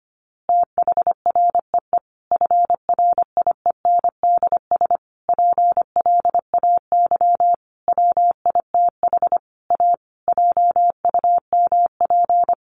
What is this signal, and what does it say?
Morse code, heard as a keyed tone, about 25 wpm, 700 Hz. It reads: THREE FRIENDS PLAY WITH A JUMP
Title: Easy Phrases 25wpm